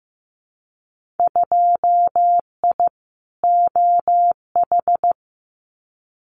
Morse code, heard as a keyed tone, about 15 wpm, 700 Hz. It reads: 2I OH
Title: Winter Field Day Exchanges 15wpm